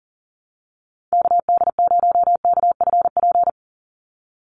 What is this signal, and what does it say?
Morse code, heard as a keyed tone, about 40 wpm, 700 Hz. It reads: KD0KFP
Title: USA Call Signs 40wpm